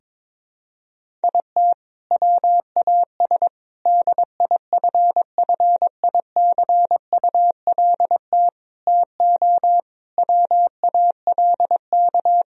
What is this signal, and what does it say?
Morse code, heard as a keyed tone, about 22 wpm, 700 Hz. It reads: IT WAS DIFFICULT TO WALK
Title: Sentences from Top 1000 Words - Review Part 1 - 22wpm